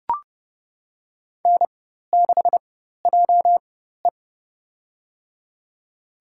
Morse code, heard as a keyed tone, about 30 wpm, spaced as for 14 wpm, 700 Hz. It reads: N6JE
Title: USA Call Signs - 4 Characters - 4x Character Spacing - 30wpm